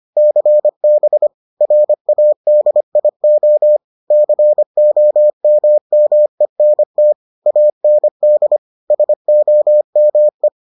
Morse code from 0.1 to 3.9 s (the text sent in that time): CB RADIO